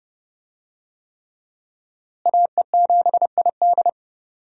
Morse code, heard as a keyed tone, about 30 wpm, 700 Hz. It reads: AE7ID
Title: USA Call Signs - 5 Characters - Repeated 3x - 30wpm